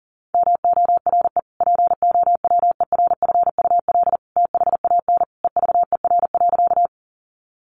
Morse code, heard as a keyed tone, about 37 wpm, 700 Hz. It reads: MORE POWERFUL THAN EVER.
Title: Sentences from Top 1300 Words 40wpm